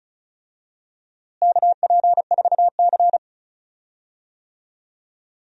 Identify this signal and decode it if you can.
Morse code, heard as a keyed tone, about 36 wpm, 700 Hz. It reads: KP4C
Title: DXCC-100 Call Signs - 4 Characters - Repeated 3x - 35wpm